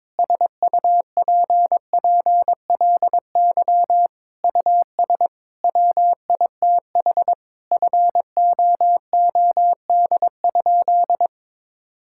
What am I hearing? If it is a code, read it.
Morse code, heard as a keyed tone, about 21 wpm, 700 Hz. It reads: SUPPLY US WITH FOOD?
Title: Sentences from Top 1000 Words - Review Part 5 - 22wpm